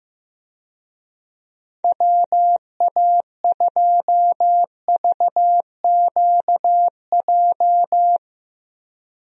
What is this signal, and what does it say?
Morse code, heard as a keyed tone, about 15 wpm, 700 Hz. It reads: WA2VQJ
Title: USA Call Signs - 6 Characters - 15wpm